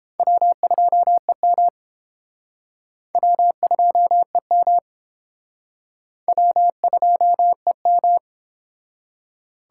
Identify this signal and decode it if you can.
Morse code, heard as a keyed tone, about 29 wpm, 700 Hz. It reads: W2EM W2EM W2EM